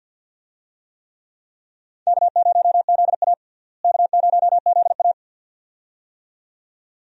Morse code, heard as a keyed tone, about 52 wpm, 700 Hz. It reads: K0ZA K0ZA
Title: USA Call Signs - 4 Characters - Repeated 2x - 50wpm